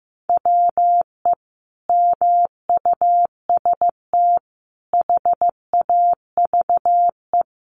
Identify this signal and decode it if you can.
Morse code, heard as a keyed tone, about 15 wpm, 700 Hz. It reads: WE MUST HAVE
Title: Sentences from Top 1300 Words 15wpm